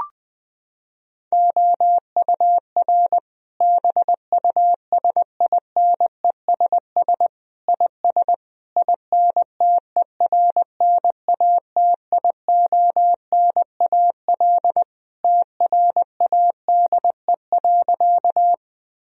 Morse code, heard as a keyed tone, about 21 wpm, 700 Hz. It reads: OUR BUSINESS IS INTERNATIONAL TRADE.
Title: Sentences from Top 1000 Words - Review Part 4 - 20wpm